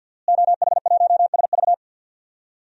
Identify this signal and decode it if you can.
Morse code, heard as a keyed tone, about 48 wpm, 700 Hz. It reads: OH1SV
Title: DXCC-100 Call Signs - 5 Characters - Repeated 2x - 50wpm